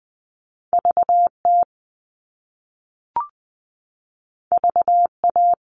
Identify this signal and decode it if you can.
Morse code, heard as a keyed tone, about 20 wpm, 700 Hz. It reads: VT VA
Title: US States 20wpm